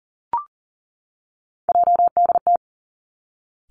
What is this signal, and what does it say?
Morse code, heard as a keyed tone, about 42 wpm, 700 Hz. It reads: JDT
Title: Three Challenging Letter-Number Combinations 40wpm